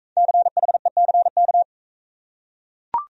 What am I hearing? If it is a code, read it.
Morse code, heard as a keyed tone, about 39 wpm, 700 Hz. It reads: CHECK